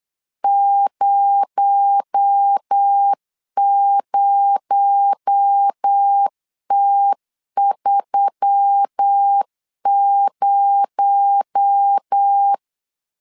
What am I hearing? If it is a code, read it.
Morse code, heard as a keyed tone, about 8 wpm, 800 Hz. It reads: 00T30
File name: LEZIONI / 6 PKX0 / 6 PKX0 8wpm 3341